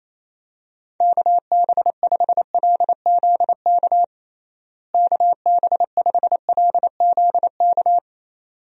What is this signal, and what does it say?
Morse code, heard as a keyed tone, about 28 wpm, 700 Hz. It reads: KB5LZK KB5LZK
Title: USA Call Signs - 6 Characters - Repeated 2x - 28wpm